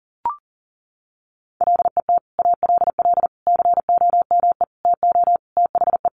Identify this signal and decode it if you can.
Morse code, heard as a keyed tone, about 40 wpm, 700 Hz. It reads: LET ALL COME TO THE
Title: Sentences from Top 1100 Words 40wpm